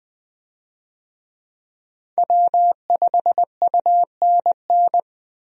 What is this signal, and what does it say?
Morse code, heard as a keyed tone, about 21 wpm, 700 Hz. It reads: W5UNN
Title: USA Call Signs - 5 Characters - 20wpm